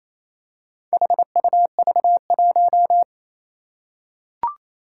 Morse code, heard as a keyed tone, about 29 wpm, 700 Hz. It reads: HUV1
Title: Four Challenging Letter-Number Combinations 28wpm